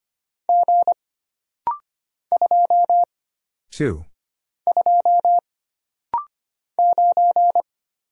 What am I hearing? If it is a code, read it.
Morse code, heard as a keyed tone, about 25 wpm, 700 Hz. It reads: G 2 2 9
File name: Single Character (Letter:Number:Punctuation:Prosign) - Rapid-Fire - 25wpm